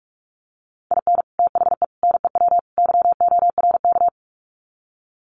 Morse code, heard as a keyed tone, about 44 wpm, 700 Hz. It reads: IN THE NEW YORK
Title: Sets of 4 Words 45wpm